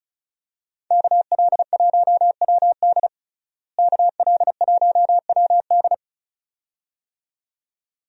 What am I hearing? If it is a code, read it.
Morse code, heard as a keyed tone, about 34 wpm, 700 Hz. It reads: KL1WD KL1WD
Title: USA Call Signs - 5 Characters - Repeated 2x - 35wpm